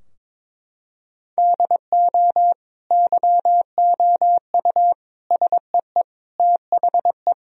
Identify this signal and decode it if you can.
Morse code, heard as a keyed tone, about 22 wpm, 700 Hz. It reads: DO YOU SEE THE
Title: Sentences from Top 1000 Words - Review Part 4 - 22wpm